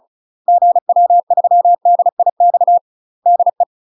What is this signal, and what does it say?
Morse code, heard as a keyed tone, about 34 wpm, 700 Hz. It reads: GW3DIX DE
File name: UK MASTER QSOs TRAINING / 35WPM / UK MASTER QSOs 0017